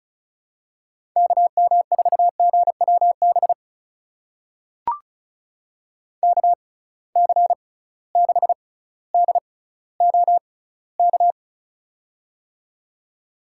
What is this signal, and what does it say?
Morse code, heard as a keyed tone, about 36 wpm, spaced as for 18 wpm, 690 Hz. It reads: KM4GWB KC6DOK
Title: USA Call Signs - 6 Characters - 6x Character Spacing - 35wpm